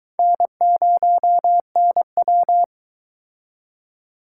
Morse code, heard as a keyed tone, about 24 wpm, 700 Hz. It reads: N0NW